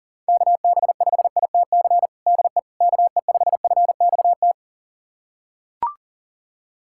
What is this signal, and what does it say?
Morse code, heard as a keyed tone, about 37 wpm, 700 Hz. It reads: KB5ITC DE KE5FXT
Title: USA Callsign DE Callsign - 6 Characters - 40wpm